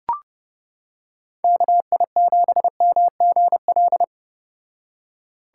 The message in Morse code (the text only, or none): KI7MGL